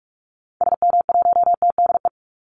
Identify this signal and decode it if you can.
Morse code, heard as a keyed tone, about 44 wpm, 700 Hz. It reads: SM1TDE
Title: Call Signs Repeated 3x - 45wpm